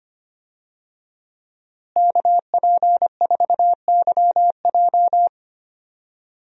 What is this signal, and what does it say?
Morse code, heard as a keyed tone, about 25 wpm, 700 Hz. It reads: KP4YJ
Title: DXCC-100 Call Signs - 5 Characters - Repeated 2x - 25wpm